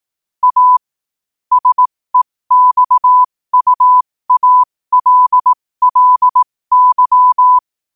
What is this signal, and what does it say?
Morse code, heard as a keyed tone, about 18 wpm, spaced as for 15 wpm, 1000 Hz. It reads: A SEXUALLY
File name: qotdfiles / 15wpm